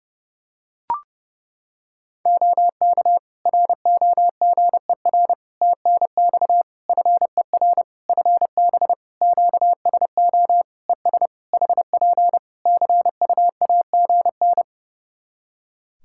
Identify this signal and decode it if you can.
Morse code, heard as a keyed tone, about 30 wpm, 700 Hz. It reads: OK ROGER TNX FER FB QSO ES HP CUAGN
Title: Practice QSO - Exchange 4 - Ending - 30wpm